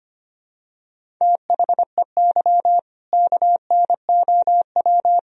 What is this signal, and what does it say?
Morse code, heard as a keyed tone, about 25 wpm, 700 Hz. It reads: THEY KNOW
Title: Sentences from Top 500 Words and Abbreviations 25wpm